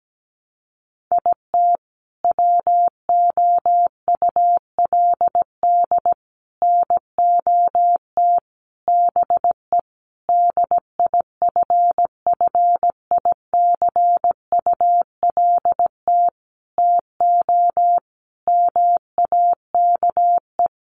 Morse code, heard as a keyed tone, about 17 wpm, 700 Hz. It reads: IT WOULD NOT BE DIFFICULT TO MAKE
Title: Sentences from Top 600 Words 17wpm